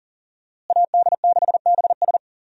Morse code, heard as a keyed tone, about 40 wpm, 700 Hz. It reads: AD6BS
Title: USA Call Signs - 5 Characters - Repeated 3x - 40wpm